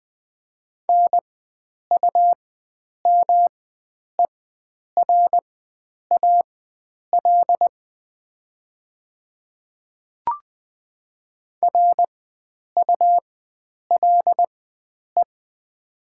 Morse code, heard as a keyed tone, about 20 wpm, spaced as for 9 wpm, 700 Hz. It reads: NUMERAL RULE
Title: 4x Character Spacing - Top 500 Words - 20wpm